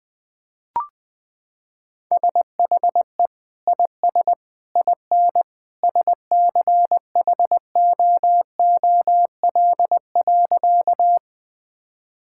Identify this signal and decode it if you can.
Morse code, heard as a keyed tone, about 21 wpm, 700 Hz. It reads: SHE IS IN SCHOOL.
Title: Sentences with 2-Word Phrases - 20wpm